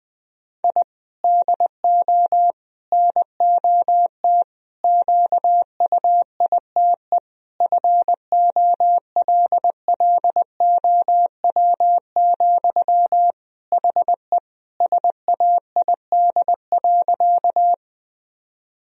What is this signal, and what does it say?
Morse code, heard as a keyed tone, about 20 wpm, 700 Hz. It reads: I DO NOT QUITE FOLLOW, HE SAID.
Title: Sentences from Top 1000 Words - Review Part 1 - 20wpm